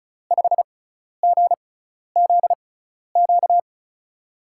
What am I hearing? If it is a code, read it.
Morse code, heard as a keyed tone, about 36 wpm, spaced as for 12 wpm, 700 Hz. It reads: 5GZQ
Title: USA Call Signs - 6 Characters - 6x Character Spacing - 35wpm